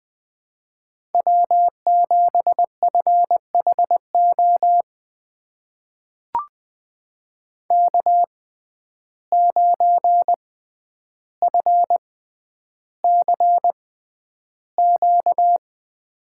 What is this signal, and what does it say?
Morse code, heard as a keyed tone, about 20 wpm, spaced as for 10 wpm, 700 Hz. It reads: W7FHO K9FCQ
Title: USA Call Signs - 5 Characters - 6x Character Spacing - 20wpm